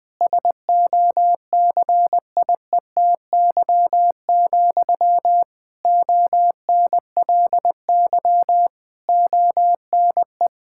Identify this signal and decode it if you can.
Morse code, heard as a keyed tone, about 19 wpm, 700 Hz. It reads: SOCIETY, ONLY ONE
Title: Sentences from Top 1000 Words - Review Part 6 - 20wpm